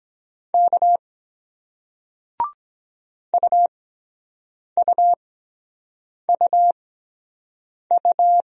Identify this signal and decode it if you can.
Morse code, heard as a keyed tone, about 23 wpm, spaced as for 7 wpm, 700 Hz. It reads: K UUUU